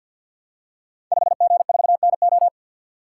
Morse code, heard as a keyed tone, about 53 wpm, 700 Hz. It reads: 5G4NO
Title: Code Groups - 5 Letter:Numbers - 50wpm